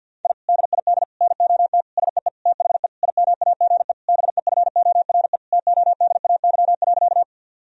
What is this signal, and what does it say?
Morse code, heard as a keyed tone, about 50 wpm, 700 Hz. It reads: I DID NOT SEE THE IMAGE BEFORE TODAY.
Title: Sentences from Top 1000 Words - Review Part 4 - 50wpm